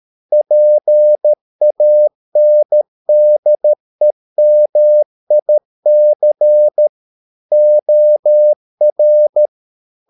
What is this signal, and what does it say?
Morse code, heard as a keyed tone, about 13 wpm, 600 Hz. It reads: PANDEMIC OR